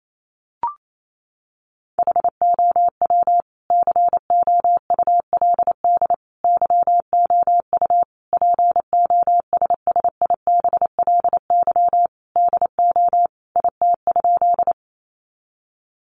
Morse code, heard as a keyed tone, about 28 wpm, 700 Hz. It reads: HOW COULD YOU POSSIBLY DO IT?
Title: Sentences from Top 1300 Words 28wpm